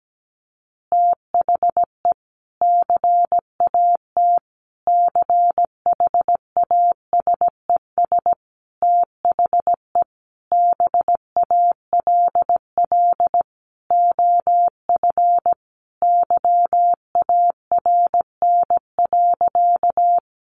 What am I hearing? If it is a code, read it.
Morse code, heard as a keyed tone, about 17 wpm, 700 Hz. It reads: THE CAT CHASES THE BALL OF YARN.
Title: Additional Easy Phrases 17wpm